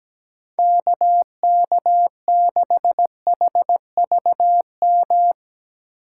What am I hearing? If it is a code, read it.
Morse code, heard as a keyed tone, about 17 wpm, 700 Hz. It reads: KK6HVM
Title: USA Call Signs - 6 Characters - 4x Character Spacing - 17wpm